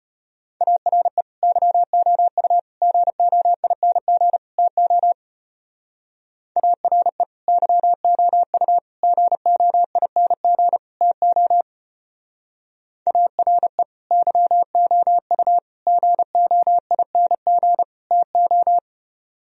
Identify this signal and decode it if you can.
Morse code, heard as a keyed tone, about 31 wpm, 700 Hz. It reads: ARE YOU GOING TO ARE YOU GOING TO ARE YOU GOING TO
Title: Speed-Racing - Sets of 4 Words 25wpm